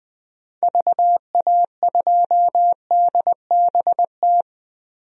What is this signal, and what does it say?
Morse code, heard as a keyed tone, about 20 wpm, 700 Hz. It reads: VA2DBT
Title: DXCC-100 Call Signs - 6 Characters - Repeated 2x - 20wpm